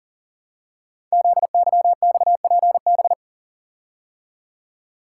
Morse code, heard as a keyed tone, about 41 wpm, 690 Hz. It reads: ZYXPB
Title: Code Groups - 5 Letter:Numbers - 40wpm